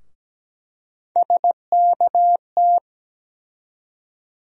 Morse code, heard as a keyed tone, about 17 wpm, 700 Hz. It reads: SKT
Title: USA Call Sign Postfixes - 3 Letters 17wpm